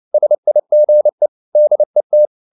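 Morse code, heard as a keyed tone, about 29 wpm, 600 Hz. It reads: SIGE DET
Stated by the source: Fast (QRQ) Morse at 29 wpm, from CW med Gnister.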